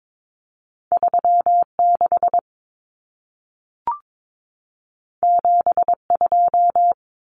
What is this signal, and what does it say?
Morse code, heard as a keyed tone, about 22 wpm, 700 Hz. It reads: 36 72
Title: Common Cut Numbers - 2-Digit 22wpm